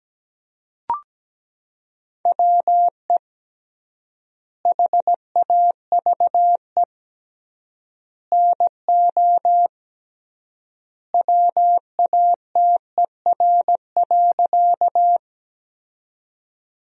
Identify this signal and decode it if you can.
Morse code, heard as a keyed tone, about 17 wpm, 700 Hz. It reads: WE HAVE NO WATER.
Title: 3x Word-Spacing - Sentences from Top 100 Words - 17wpm